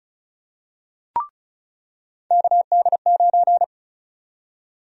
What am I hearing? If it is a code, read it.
Morse code, heard as a keyed tone, about 34 wpm, 700 Hz. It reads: KD9
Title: USA Call Sign Prefixes - 35wpm